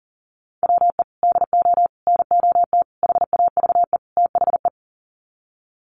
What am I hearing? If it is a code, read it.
Morse code, heard as a keyed tone, about 40 wpm, 700 Hz. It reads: WE DO NOT HAVE THE
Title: Sets of 5 Words 40wpm